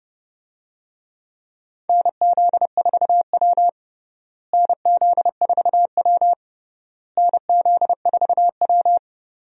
Morse code, heard as a keyed tone, about 30 wpm, 700 Hz. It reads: NZ4W NZ4W NZ4W
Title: USA Call Signs - 4 Characters - Repeated 3x - 30wpm